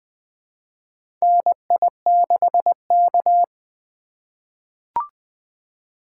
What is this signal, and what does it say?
Morse code, heard as a keyed tone, about 20 wpm, 700 Hz. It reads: NI6K
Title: USA Call Signs - 4 Characters - 3x Character Spacing - 20wpm